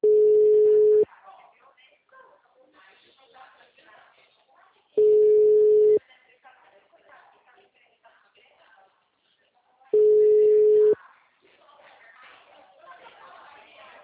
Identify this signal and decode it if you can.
Ringback tone (Europe, 425 Hz), with speech.